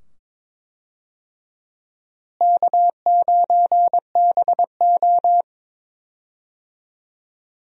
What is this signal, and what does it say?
Morse code, heard as a keyed tone, about 21 wpm, 700 Hz. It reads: K9BO
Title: USA Call Signs - 4 Characters - Repeated 2x - 22wpm